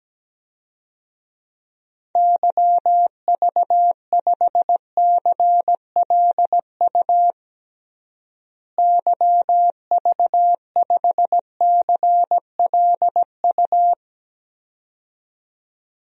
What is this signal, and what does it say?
Morse code, heard as a keyed tone, about 17 wpm, 700 Hz. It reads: YV5CLU YV5CLU
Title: DXCC-100 Call Signs - 6 Characters - Repeated 2x - 17wpm